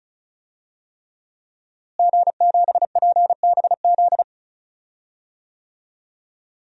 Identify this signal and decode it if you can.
Morse code, heard as a keyed tone, about 34 wpm, 700 Hz. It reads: G7PBZ
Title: DXCC-100 Call Signs - 5 Characters - Repeated 3x - 35wpm